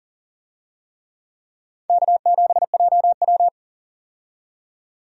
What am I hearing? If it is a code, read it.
Morse code, heard as a keyed tone, about 40 wpm, 700 Hz. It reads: K7JW
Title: USA Call Signs - 4 Characters - 40wpm